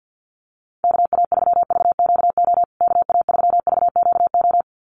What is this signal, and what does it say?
Morse code, heard as a keyed tone, about 50 wpm, 700 Hz. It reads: KA3VQO KA3VQO
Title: USA Call Signs Repeated 2x 50wpm